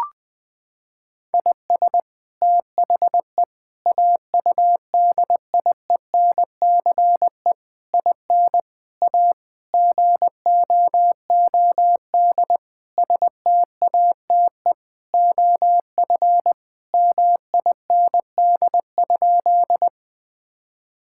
Morse code, heard as a keyed tone, about 20 wpm, 700 Hz. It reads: IS THE AUDIENCE IN A GOOD STATE OF MIND?
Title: Sentences from Top 1000 Words - Review Part 5 - 20wpm